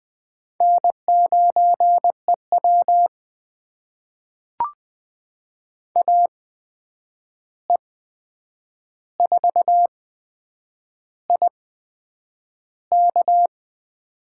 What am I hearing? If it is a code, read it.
Morse code, heard as a keyed tone, about 21 wpm, spaced as for 8 wpm, 700 Hz. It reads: N9EW AE4IK
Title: USA Call Signs - 8x Character Spacing - 20wpm